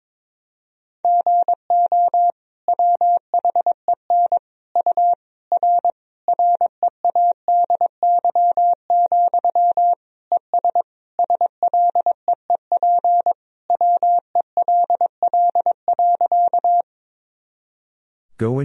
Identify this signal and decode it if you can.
Morse code, heard as a keyed tone, about 22 wpm, 700 Hz. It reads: GO WHEN U R READY, ES SLEEP WELL.
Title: Sentences from Top 500 Words and Abbreviations 22wpm